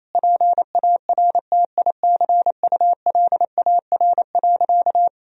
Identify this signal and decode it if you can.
Morse code, heard as a keyed tone, about 28 wpm, 700 Hz. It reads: PARTICULAR.
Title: Sentences from Top 1000 Words - Review Part 4 - 28wpm